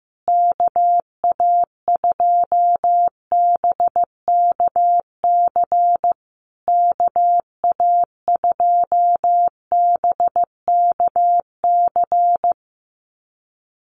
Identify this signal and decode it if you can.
Morse code, heard as a keyed tone, about 14 wpm, 700 Hz. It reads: KA2BKC KA2BKC
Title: USA Call Signs Repeated 3x 15wpm